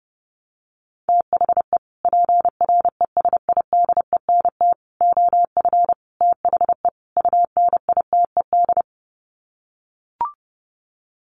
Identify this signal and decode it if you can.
Morse code, heard as a keyed tone, about 30 wpm, 700 Hz. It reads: THE PRESIDENT OF THE UNITED
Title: Sets of 5 Words 30wpm